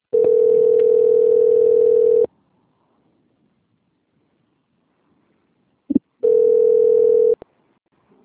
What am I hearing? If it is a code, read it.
Ringback tone (North America), 440 + 480 Hz.
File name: out- - - - - .36182